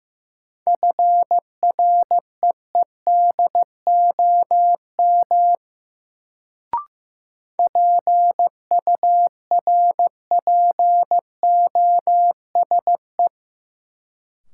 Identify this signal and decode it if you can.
Morse code, heard as a keyed tone, about 15 wpm, 700 Hz. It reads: FREEDOM PURPOSE
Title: Sentences from Top 800 Words - New Word Drill - 15wpm